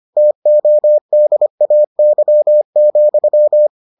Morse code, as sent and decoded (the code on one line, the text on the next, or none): - --- -.. .- -.-- --..--
TODAY,